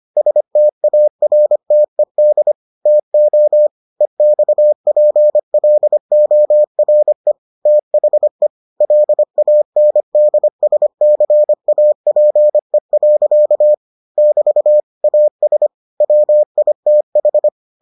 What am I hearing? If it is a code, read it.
Morse code, heard as a keyed tone, about 25 wpm, 600 Hz. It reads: STARTED TO EXPLORE THE LANDSCAPE. <BT> AS WITH